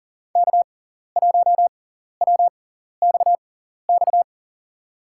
Morse code, heard as a keyed tone, about 40 wpm, spaced as for 14 wpm, 700 Hz. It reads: K1WXX
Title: USA Call Signs - 5 Characters - 6x Character Spacing - 40wpm